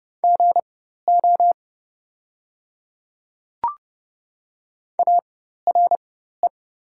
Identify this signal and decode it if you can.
Morse code, heard as a keyed tone, about 30 wpm, spaced as for 14 wpm, 700 Hz. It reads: GO ARE
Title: 4x Character Spacing - Top 100 Words - 30wpm